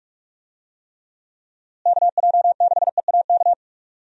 Morse code, heard as a keyed tone, about 43 wpm, 700 Hz. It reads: KJ6EAK
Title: USA Call Signs - 6 Characters - Repeated 2x - 45wpm